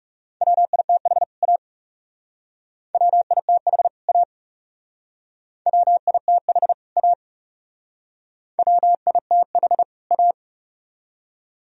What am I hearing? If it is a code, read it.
Morse code, heard as a keyed tone, about 36 wpm, 700 Hz. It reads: WITH A WITH A WITH A WITH A
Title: Speed-Racing - Sets of 2 Words 30wpm